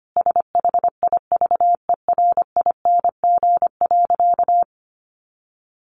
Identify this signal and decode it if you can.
Morse code, heard as a keyed tone, about 25 wpm, 700 Hz. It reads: SHIVERING.
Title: A Sound Of Thunder 25wpm